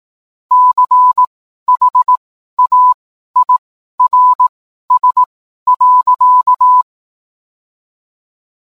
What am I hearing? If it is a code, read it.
Morse code, heard as a keyed tone, about 18 wpm, spaced as for 12 wpm, 1000 Hz. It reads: CHAIRS.